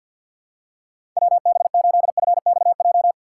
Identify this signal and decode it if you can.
Morse code, heard as a keyed tone, about 51 wpm, 700 Hz. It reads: WB8FXJ
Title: K1USN SST QSO Practice - 50wpm